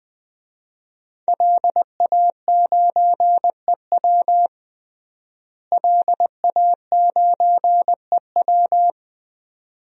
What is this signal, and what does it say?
Morse code, heard as a keyed tone, about 20 wpm, 700 Hz. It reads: LA9EW LA9EW
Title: DXCC-100 Call Signs - 5 Characters - Repeated 2x - 20wpm